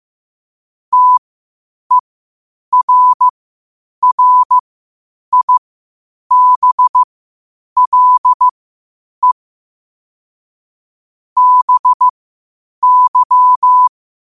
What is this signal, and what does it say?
Morse code, heard as a keyed tone, about 15 wpm, spaced as for 8 wpm, 1000 Hz. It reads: TERRIBLE BY
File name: qotdfiles / 8wpm